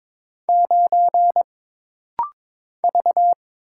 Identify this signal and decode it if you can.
Morse code, heard as a keyed tone, about 22 wpm, 700 Hz. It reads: 9 V
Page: This rapid-fire timing is geared to push your ICR (Instant Character Recognition) skills.